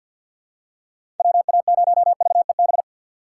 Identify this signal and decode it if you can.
Morse code, heard as a keyed tone, about 53 wpm, 700 Hz. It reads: WA0VEB
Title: USA Call Signs - 6 Characters - 3x Character Spacing - 50wpm